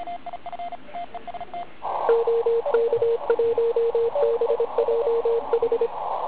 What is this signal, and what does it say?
Morse code, heard as a keyed tone, about 25 wpm, 460 Hz, with radio noise. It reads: OK1BJH
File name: Archiv / CUC138 / OK1IF CUC138